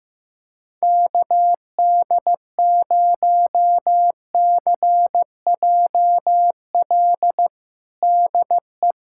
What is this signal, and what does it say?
Morse code, heard as a keyed tone, about 16 wpm, 700 Hz. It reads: KD0CJL DE
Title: USA Callsign DE Callsign 15wpm